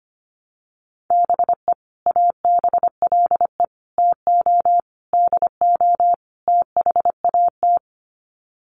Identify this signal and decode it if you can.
Morse code, heard as a keyed tone, about 25 wpm, 700 Hz. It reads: BE ABLE TO DO THAT
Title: Sets of 5 Words Encore2 25wpm